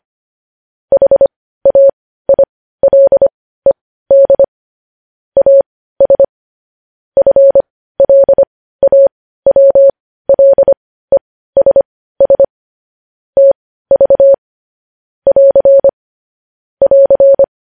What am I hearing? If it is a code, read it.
Morse code, heard as a keyed tone, about 25 wpm, spaced as for 15 wpm, 550 Hz. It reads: HAILED AS FLAWLESS TV <AR> <AR>
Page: news-headlines-in-morse-code-at-15-wpm